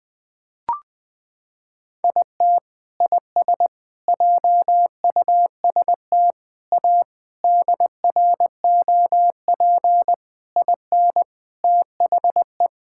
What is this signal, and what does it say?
Morse code, heard as a keyed tone, about 20 wpm, 700 Hz. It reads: IT IS JUST A DROP IN THE
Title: Sentences with 8-Word Phrases - 20wpm